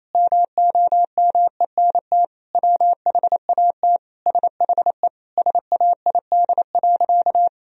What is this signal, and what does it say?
Morse code, heard as a keyed tone, about 28 wpm, 700 Hz. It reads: MOMENT WHAT SHE SAID.
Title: Sentences from Top 1000 Words - Review Part 4 - 28wpm